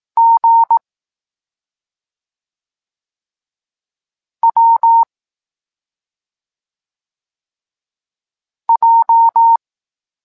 Morse code, heard as a keyed tone, about 18 wpm, 920 Hz. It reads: G W J
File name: Tekst-26 3